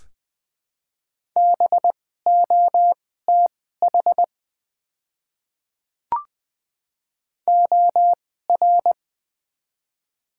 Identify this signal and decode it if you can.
Morse code, heard as a keyed tone, about 20 wpm, spaced as for 14 wpm, 700 Hz. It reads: BOTH OR
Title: 2x Character Spacing - Top 300 Words - 20wpm